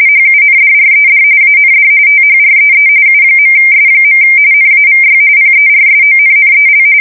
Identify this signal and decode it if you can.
RTTY, about 45.45 baud. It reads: HUNDRED AND THREE U.S. NAVY PERSONNEL
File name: fsk170 / usnavy5